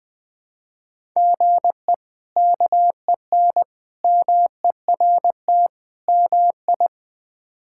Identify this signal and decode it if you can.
Morse code, heard as a keyed tone, about 20 wpm, 700 Hz. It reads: GE KEN MERT MI
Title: K1USN SST QSO Practice - 20wpm